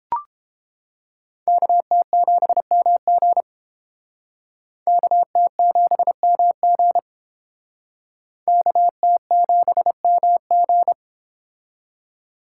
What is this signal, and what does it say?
Morse code, heard as a keyed tone, about 30 wpm, 700 Hz. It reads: KT7MG KT7MG KT7MG